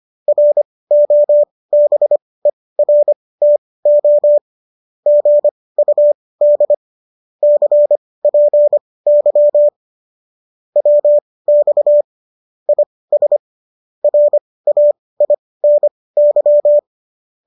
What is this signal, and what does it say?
Morse code, heard as a keyed tone, about 25 wpm, spaced as for 18 wpm, 590 Hz. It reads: ROBERTO GUD CPY WX IS RAINY
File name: qso217 18